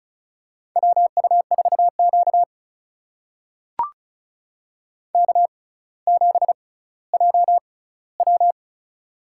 Morse code, heard as a keyed tone, about 34 wpm, spaced as for 18 wpm, 700 Hz. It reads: WU4Q K7JW
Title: USA Call Signs - 4 Characters - 6x Character Spacing - 35wpm